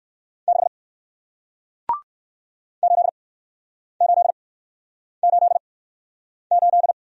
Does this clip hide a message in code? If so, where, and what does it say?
Yes: Morse code from 0.5 to 6.9 s, heard as a keyed tone, about 56 wpm, spaced as for 9 wpm, 700 Hz. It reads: 6 8888